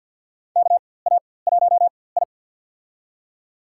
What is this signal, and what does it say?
Morse code, heard as a keyed tone, about 50 wpm, spaced as for 23 wpm, 700 Hz. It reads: KA1I
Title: USA Call Signs - 4x Character Spacing - 50wpm